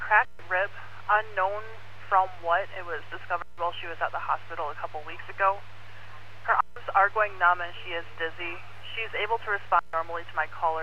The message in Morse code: none

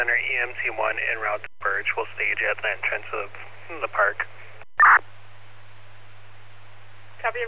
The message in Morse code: none

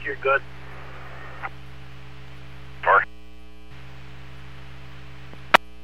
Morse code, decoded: none

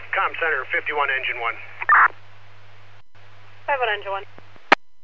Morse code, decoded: none

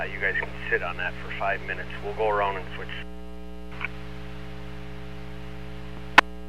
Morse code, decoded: none

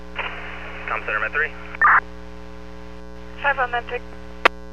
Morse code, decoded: none

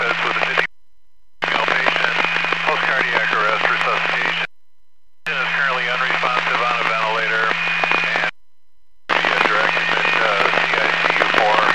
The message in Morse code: none

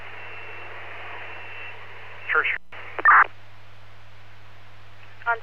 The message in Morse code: none

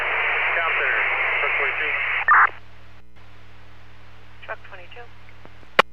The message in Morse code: none

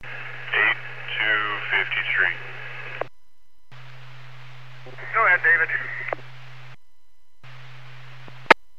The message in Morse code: none

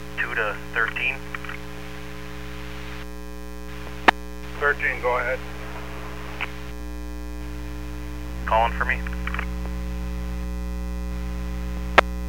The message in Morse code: none

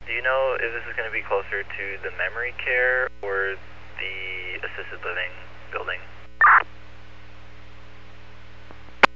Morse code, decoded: none

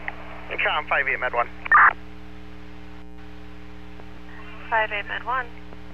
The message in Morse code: none

